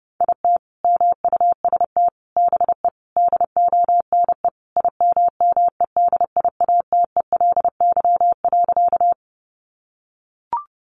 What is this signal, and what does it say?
Morse code, heard as a keyed tone, about 30 wpm, 700 Hz. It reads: IT MUST BE DONE IMMEDIATELY.
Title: Sentences from Top 1100 Words 30wpm